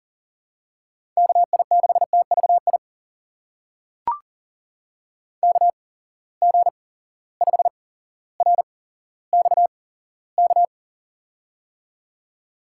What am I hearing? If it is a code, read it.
Morse code, heard as a keyed tone, about 40 wpm, spaced as for 17 wpm, 700 Hz. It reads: KI6TVI KG5RXK
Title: USA Call Signs - 6 Characters - 8x Character Spacing - 40wpm